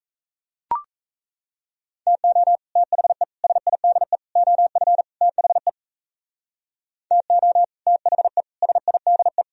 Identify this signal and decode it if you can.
Morse code, heard as a keyed tone, about 40 wpm, 700 Hz. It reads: TO THE SIDE OF THE TO THE SIDE